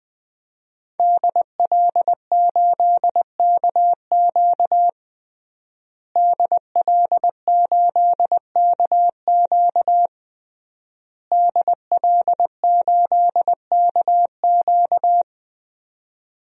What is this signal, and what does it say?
Morse code, heard as a keyed tone, about 20 wpm, 700 Hz. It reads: DL8KQ DL8KQ DL8KQ
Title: DXCC-100 Call Signs - 5 Characters - Repeated 3x - 20wpm